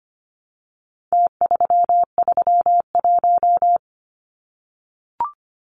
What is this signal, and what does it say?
Morse code, heard as a keyed tone, about 25 wpm, 700 Hz. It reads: T331
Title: Numbers - Contest Style - 4-Digit 25wpm